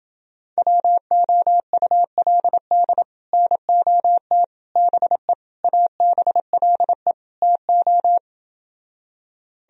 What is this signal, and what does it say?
Morse code, heard as a keyed tone, about 30 wpm, 700 Hz. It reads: WOULD NOT BE ABLE TO